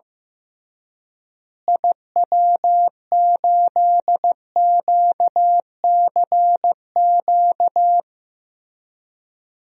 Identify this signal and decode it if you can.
Morse code, heard as a keyed tone, about 16 wpm, 700 Hz. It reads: IW8QCQ
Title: DXCC-100 Call Signs - 6 Characters - Repeated 3x - 15wpm